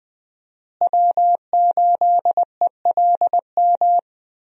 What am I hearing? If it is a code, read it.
Morse code, heard as a keyed tone, about 21 wpm, 700 Hz. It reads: W8ELM
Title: USA Call Signs - 5 Characters - Repeated 2x - 20wpm